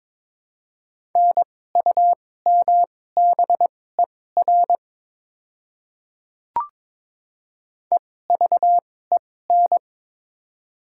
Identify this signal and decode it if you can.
Morse code, heard as a keyed tone, about 21 wpm, spaced as for 16 wpm, 700 Hz. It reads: NUMBER EVEN
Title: 2x Character Spacing - Top 200 Words - 22wpm